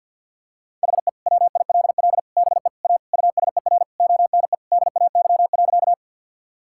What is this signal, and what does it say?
Morse code, heard as a keyed tone, about 50 wpm, 700 Hz. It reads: HE WILL BE A USER ONE DAY.
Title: Sentences from Top 1000 Words - Review Part 5 - 50wpm